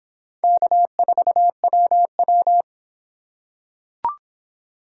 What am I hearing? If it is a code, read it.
Morse code, heard as a keyed tone, about 25 wpm, 700 Hz. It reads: K4WW